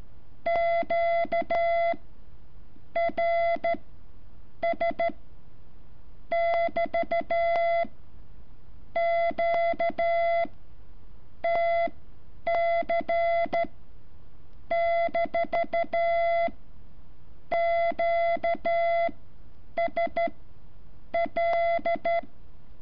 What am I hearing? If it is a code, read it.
Morse code, heard as a keyed tone, about 13 wpm, spaced as for 7 wpm, 680 Hz. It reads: QRS<BT>QTC-QSL
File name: curso / ejercicio 14